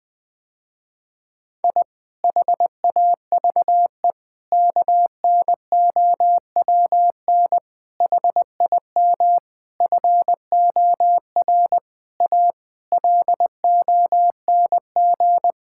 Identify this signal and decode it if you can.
Morse code, heard as a keyed tone, about 20 wpm, 700 Hz. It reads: I HAVE KNOWN HIM FOR A LONG
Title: Sentences with 4-Word Phrases - Spoken Beginning - 20wpm